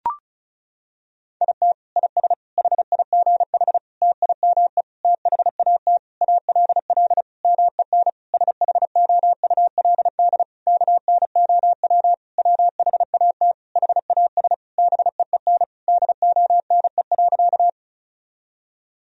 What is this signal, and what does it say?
Morse code, heard as a keyed tone, about 36 wpm, 700 Hz. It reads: IT IS HIGH TIME THAT ALL MEN SHOULD KNOW WHAT HAS BEEN DONE.
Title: Sentences from Top 1000 Words - Review Part 1 - 35wpm